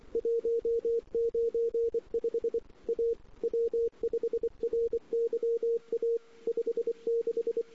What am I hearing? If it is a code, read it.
Morse code, heard as a keyed tone, about 25 wpm, 440 Hz. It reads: 195 A W5RYA 56
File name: CW / 2008-ss-cw-k5tr-op-n5rz -04:00